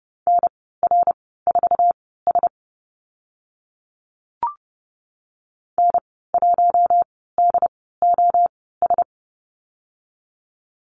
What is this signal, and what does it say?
Morse code, heard as a keyed tone, about 30 wpm, spaced as for 17 wpm, 700 Hz. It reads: NR4S N1DOS
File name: Lesson 14g - Number 2 - Call Signs 30wpm